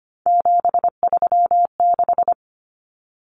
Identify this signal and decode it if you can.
Morse code, heard as a keyed tone, about 25 wpm, 700 Hz. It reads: 736
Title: Common Cut Numbers - 3-Digit 25wpm